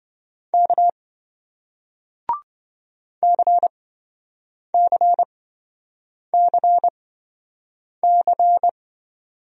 Morse code, heard as a keyed tone, about 28 wpm, spaced as for 7 wpm, 700 Hz. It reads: K CCCC